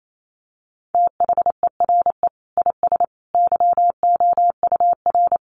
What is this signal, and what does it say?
Morse code, heard as a keyed tone, about 28 wpm, 700 Hz. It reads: THERE IS YOUR
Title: Sentences from Top 800 Words 28wpm